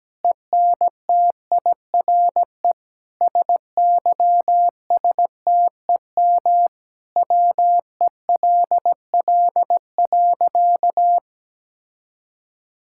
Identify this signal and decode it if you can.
Morse code, heard as a keyed tone, about 17 wpm, 700 Hz. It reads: ENTIRE SYSTEM WELL.
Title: Sentences from Top 1000 Words - Review Part 5 - 17wpm